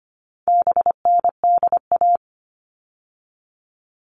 Morse code, heard as a keyed tone, about 25 wpm, 700 Hz. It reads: BNDA
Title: All Cut Numbers - 4-Digit 25wpm